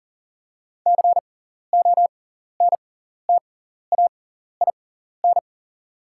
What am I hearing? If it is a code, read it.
Morse code, heard as a keyed tone, about 38 wpm, spaced as for 14 wpm, 700 Hz. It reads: CONTAIN
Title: 6x Character Spacing - Top 500 Words - 40wpm